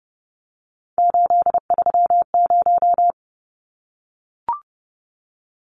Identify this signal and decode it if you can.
Morse code, heard as a keyed tone, about 29 wpm, 700 Hz. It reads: 830
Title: Numbers - 3-Digit 30wpm